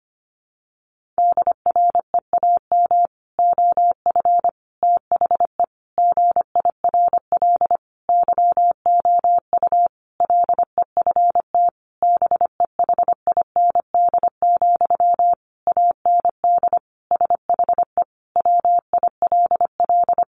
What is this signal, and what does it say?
Morse code, heard as a keyed tone, about 25 wpm, 700 Hz. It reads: DREAM OF THE GIRL YOU LEFT BEHIND, AND SHE WILL
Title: Sentences from Top 1500 Words 25wpm